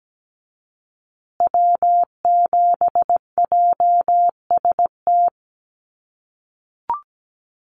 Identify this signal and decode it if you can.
Morse code, heard as a keyed tone, about 17 wpm, 700 Hz. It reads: W7JST
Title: USA Call Signs Repeated 2x 17wpm